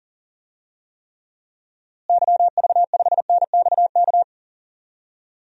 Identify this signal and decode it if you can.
Morse code, heard as a keyed tone, about 40 wpm, 700 Hz. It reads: YV5NXK
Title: DXCC-100 Call Signs - 6 Characters - Repeated 2x - 40wpm